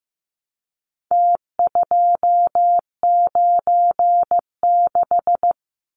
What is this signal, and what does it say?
Morse code, heard as a keyed tone, about 15 wpm, 700 Hz. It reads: T296
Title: Numbers - Contest Style - 4-Digit 15wpm